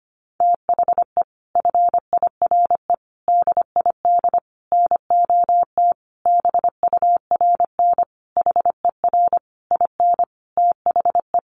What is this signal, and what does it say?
Morse code, heard as a keyed tone, about 25 wpm, 700 Hz. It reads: THE FIRE DID NOT BURN HER IN THE
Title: Sentences from Top 1300 Words 25wpm